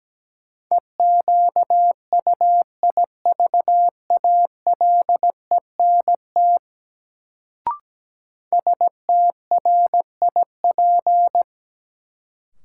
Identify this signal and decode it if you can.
Morse code, heard as a keyed tone, about 17 wpm, 700 Hz. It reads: EQUIVALENT STRIP
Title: Sentences from Top 1600 Words - New Word Drill Alt - 17wpm